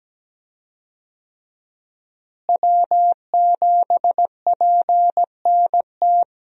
Morse code, heard as a keyed tone, about 17 wpm, 700 Hz. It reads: W7PNT
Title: USA Call Signs - 5 Characters - Repeated 2x - 17wpm